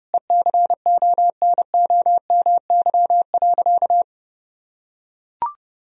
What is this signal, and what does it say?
Morse code, heard as a keyed tone, about 32 wpm, 700 Hz. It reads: ECONOMY.
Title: Sentences from Top 1000 Words - Review Part 3 - 30wpm